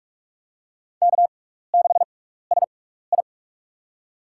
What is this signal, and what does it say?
Morse code, heard as a keyed tone, about 42 wpm, spaced as for 15 wpm, 700 Hz. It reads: K6SI
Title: USA Call Signs - 4 Characters - 6x Character Spacing - 45wpm